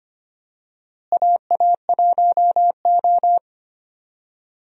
Morse code, heard as a keyed tone, about 25 wpm, 700 Hz. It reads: AA1O
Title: USA Call Signs - 4 Characters - Repeated 2x - 25wpm